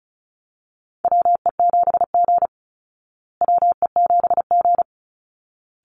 Morse code, heard as a keyed tone, about 34 wpm, 700 Hz. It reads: WE7G WE7G
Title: Call Signs Repeated 3x - 35wpm